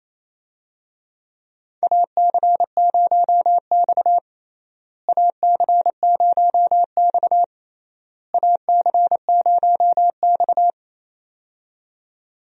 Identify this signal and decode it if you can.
Morse code, heard as a keyed tone, about 27 wpm, 700 Hz. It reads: AC0X AC0X AC0X
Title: USA Call Signs - 4 Characters - Repeated 3x - 28wpm